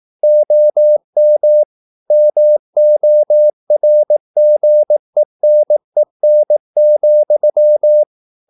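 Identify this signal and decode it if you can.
Morse code, heard as a keyed tone, about 18 wpm, 600 Hz. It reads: OM MORGENEN,
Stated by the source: Slow (QRS) Morse practice at 18 wpm, CW med Gnister.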